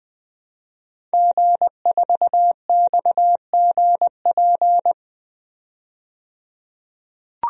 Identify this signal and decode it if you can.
Morse code, heard as a keyed tone, about 20 wpm, 700 Hz. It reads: G4XGP
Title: DXCC-100 Call Signs - 5 Characters - 20wpm